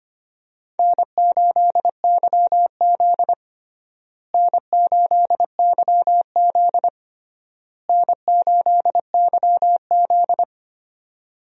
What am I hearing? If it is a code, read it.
Morse code, heard as a keyed tone, about 25 wpm, 700 Hz. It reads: N8YZ N8YZ N8YZ
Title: USA Call Signs - 4 Characters - Repeated 3x - 25wpm